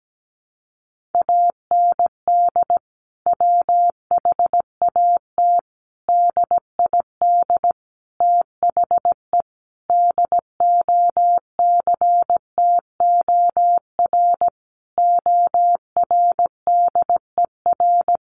Morse code, heard as a keyed tone, about 17 wpm, 700 Hz. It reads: AND WHAT DID THE DOCTOR ORDER
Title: Sentences from Top 1500 Words 17wpm